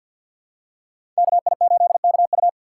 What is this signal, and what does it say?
Morse code, heard as a keyed tone, about 48 wpm, 700 Hz. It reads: KI8KU
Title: USA Call Signs - 5 Characters - 3x Character Spacing - 50wpm